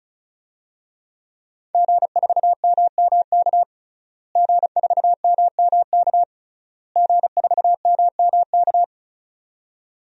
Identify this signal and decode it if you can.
Morse code, heard as a keyed tone, about 36 wpm, 700 Hz. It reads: G4MMK G4MMK G4MMK
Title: DXCC-100 Call Signs - 5 Characters - Repeated 3x - 35wpm